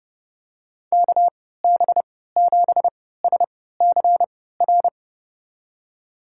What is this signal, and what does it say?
Morse code, heard as a keyed tone, about 31 wpm, spaced as for 17 wpm, 700 Hz. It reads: KB7SCR
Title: USA Call Signs - 3x Character Spacing - 30wpm